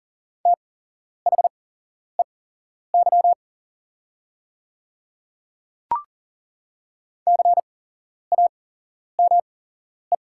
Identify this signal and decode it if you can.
Morse code, heard as a keyed tone, about 40 wpm, spaced as for 11 wpm, 700 Hz. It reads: THEY CAME
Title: 8x Character Spacing - Top 200 Words - 40wpm